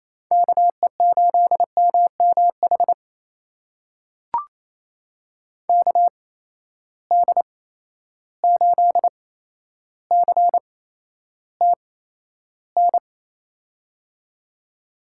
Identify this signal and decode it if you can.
Morse code, heard as a keyed tone, about 28 wpm, spaced as for 12 wpm, 700 Hz. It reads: KE8MMH KD8CTN
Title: USA Call Signs - 6 Characters - 8x Character Spacing - 28wpm